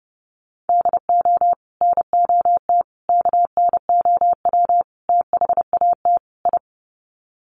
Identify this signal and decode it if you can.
Morse code, heard as a keyed tone, about 30 wpm, 700 Hz. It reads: DO NOT KNOW THAT I
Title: Sets of 5 Words Encore2 30wpm